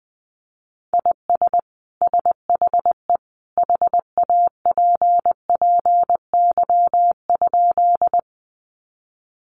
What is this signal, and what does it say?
Morse code, heard as a keyed tone, about 20 wpm, 700 Hz. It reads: IS SHE HAPPY?
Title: Sentences from Top 1000 Words 20wpm